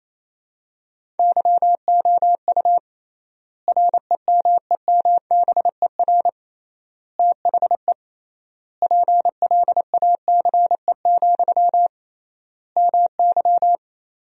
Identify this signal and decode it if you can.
Morse code, heard as a keyed tone, about 28 wpm, 700 Hz. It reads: YOU REMEMBER THE PLACE, MY
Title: 3x Word-Spacing - Sentences from Top 400 Words - 28wpm